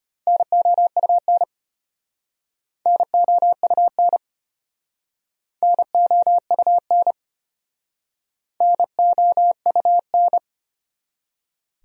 Morse code, heard as a keyed tone, about 33 wpm, 700 Hz. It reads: NOUN NOUN NOUN NOUN